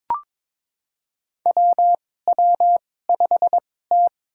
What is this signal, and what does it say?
Morse code, heard as a keyed tone, about 22 wpm, spaced as for 16 wpm, 700 Hz. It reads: WW5T
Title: USA Call Signs - 2x Character Spacing - 22wpm